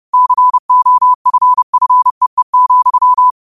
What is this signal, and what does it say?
Morse code, heard as a keyed tone, about 30 wpm, 1000 Hz. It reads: COFFEE,
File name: qotdfiles / 30wpm